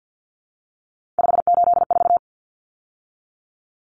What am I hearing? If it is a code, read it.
Morse code, heard as a keyed tone, about 50 wpm, 700 Hz. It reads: 584
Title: Numbers - 3-Digit 50wpm